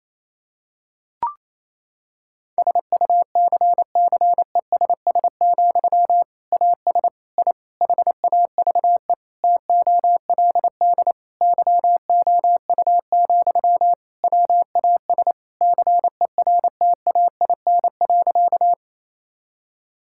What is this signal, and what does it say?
Morse code, heard as a keyed tone, about 28 wpm, 700 Hz. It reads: SUCCESS, AS I HAVE TOLD YOU, WAS CERTAIN.
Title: Sentences from Top 1000 Words - Review Part 6 - 28wpm